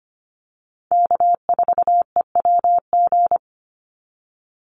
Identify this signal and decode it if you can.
Morse code, heard as a keyed tone, about 25 wpm, 700 Hz. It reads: K4EWG
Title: Call Signs Repeated 2x - 25wpm